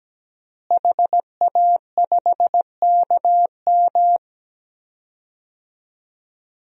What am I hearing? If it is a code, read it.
Morse code, heard as a keyed tone, about 17 wpm, 700 Hz. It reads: HA5KM
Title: DXCC-100 Call Signs - 5 Characters - Repeated 2x - 17wpm